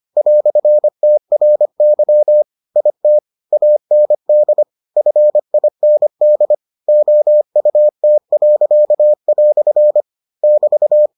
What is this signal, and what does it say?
Morse code, heard as a keyed tone, about 25 wpm, 600 Hz. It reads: "TRY IT AND FIND OUT." <BT>